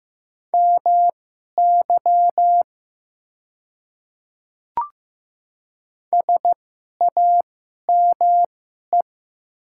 Morse code, heard as a keyed tone, about 15 wpm, spaced as for 11 wpm, 700 Hz. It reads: MY SAME
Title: 2x Character Spacing - Top 300 Words - 15wpm